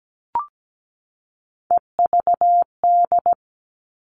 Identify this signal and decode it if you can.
Morse code, heard as a keyed tone, about 17 wpm, 700 Hz. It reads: EVD
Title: All Cut Numbers - 3-Digit 17wpm